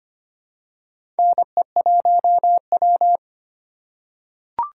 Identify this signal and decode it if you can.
Morse code, heard as a keyed tone, about 25 wpm, 700 Hz. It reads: NE1W
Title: USA Call Signs - 4 Characters - 4x Character Spacing - 25wpm